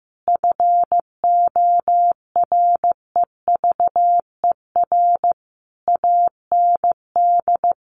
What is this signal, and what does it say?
Morse code, heard as a keyed tone, about 15 wpm, 700 Hz. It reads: FOREVER AND
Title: A Sound Of Thunder 15wpm